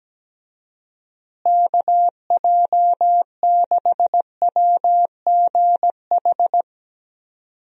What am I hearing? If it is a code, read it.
Morse code, heard as a keyed tone, about 17 wpm, 700 Hz. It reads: KJ6WGH
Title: USA Call Signs - 6 Characters - Repeated 2x - 17wpm